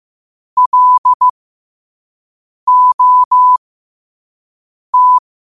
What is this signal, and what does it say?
Morse code, heard as a keyed tone, about 15 wpm, spaced as for 5 wpm, 1000 Hz. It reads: LOT
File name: qotdfiles / 5wpm